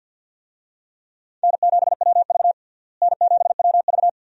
Morse code, heard as a keyed tone, about 46 wpm, 700 Hz. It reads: N7WV N7WV
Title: USA Call Signs - 4 Characters - Repeated 2x - 50wpm